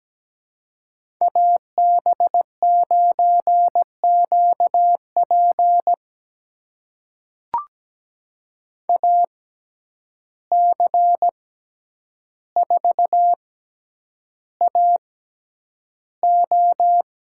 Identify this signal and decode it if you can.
Morse code, heard as a keyed tone, about 17 wpm, spaced as for 9 wpm, 700 Hz. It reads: AB9QP AC4AO
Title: USA Call Signs - 5 Characters - 6x Character Spacing - 17wpm